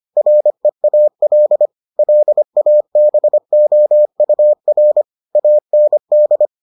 Morse code, heard as a keyed tone, about 25 wpm, 600 Hz. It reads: REAL LABOUR AND